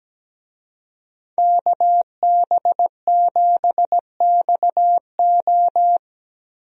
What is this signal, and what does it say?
Morse code, heard as a keyed tone, about 17 wpm, 700 Hz. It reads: KB7XO
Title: USA Call Signs - 5 Characters - Repeated 3x - 17wpm